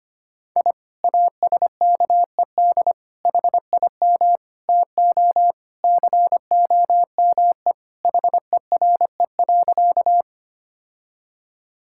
Morse code, heard as a keyed tone, about 25 wpm, 700 Hz. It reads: I ASKED HIM TO COME HERE.
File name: practice / 25wpm / Sentences with 2-Word Phrases - 25wpm